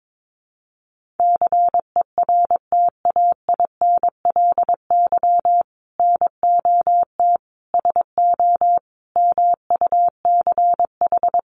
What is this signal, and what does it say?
Morse code, heard as a keyed tone, about 22 wpm, 700 Hz. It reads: CERTAINLY NOT SO MUCH
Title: Sentences from Top 1500 Words 22wpm